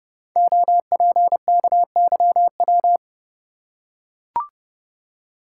Morse code, heard as a keyed tone, about 30 wpm, 700 Hz. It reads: OPKYW
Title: Code Groups - 5 Letter:Numbers - 30wpm